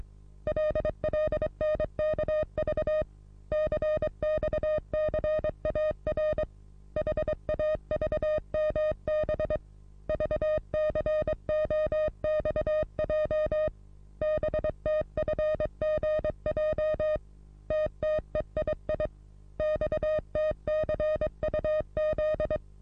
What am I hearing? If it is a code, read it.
Morse code, heard as a keyed tone, about 22 wpm, 620 Hz. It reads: LLNKV CXCAR HAVMB VCOXJ BTFGJ TTEII XTCUZ
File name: repos / Morsekurs / DJ4UF Morsen27